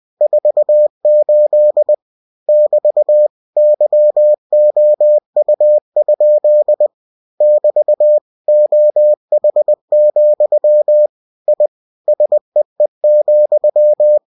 Morse code, heard as a keyed tone, about 20 wpm, 600 Hz. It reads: 48 <BT> YOU? <BT> OH, I SEE,
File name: cw-books / winnie-the-pooh / winnie-20wpm-page-0048